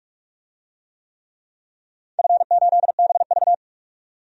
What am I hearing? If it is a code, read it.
Morse code, heard as a keyed tone, about 46 wpm, 700 Hz. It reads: F8BV
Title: DXCC-100 Call Signs - 4 Characters - 45wpm